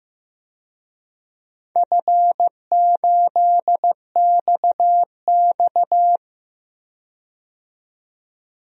Morse code, heard as a keyed tone, about 15 wpm, 700 Hz. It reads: F8XX
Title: DXCC-100 Call Signs - 4 Characters - 15wpm